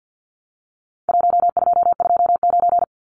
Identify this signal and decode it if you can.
Morse code, heard as a keyed tone, about 50 wpm, 700 Hz. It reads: 1229
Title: Numbers - Contest Style - 4-Digit 50wpm